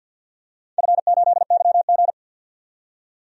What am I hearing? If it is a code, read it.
Morse code, heard as a keyed tone, about 51 wpm, 700 Hz. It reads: F8YG
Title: DXCC-100 Call Signs - 4 Characters - 50wpm